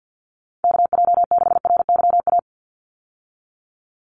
Morse code, heard as a keyed tone, about 50 wpm, 700 Hz. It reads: KJ6RYA
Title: USA Call Signs 50wpm